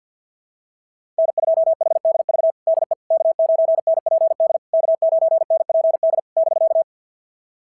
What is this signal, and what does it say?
Morse code, heard as a keyed tone, about 52 wpm, 690 Hz. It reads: N2HDV DE K9NPD K9NPD <BK>
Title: SOTA QSO Practice 50wpm